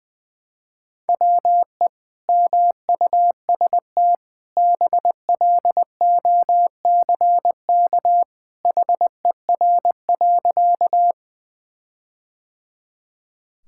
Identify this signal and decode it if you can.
Morse code, heard as a keyed tone, about 20 wpm, 700 Hz. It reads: WE MUST BLOCK HER.
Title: Sentences from Top 1600 Words 20wpm